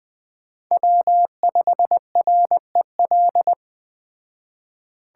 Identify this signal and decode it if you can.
Morse code, heard as a keyed tone, about 20 wpm, 700 Hz. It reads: W5REL